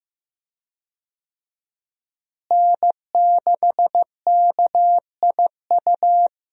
Morse code, heard as a keyed tone, about 15 wpm, 700 Hz. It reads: N6KIU
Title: USA Call Signs - 5 Characters - Repeated 3x - 15wpm